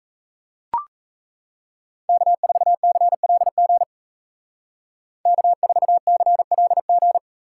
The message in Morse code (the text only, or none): K4CLG K4CLG